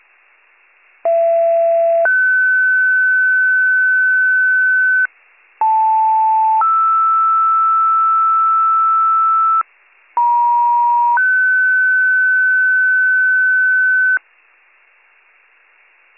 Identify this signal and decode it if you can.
Two-tone page, heard as a one-second tone, then a three-second tone, about 670 then 1500 Hz, about 880 then 1320 Hz, about 950 then 1500 Hz. It is heard from 1.0 to 14.2 s.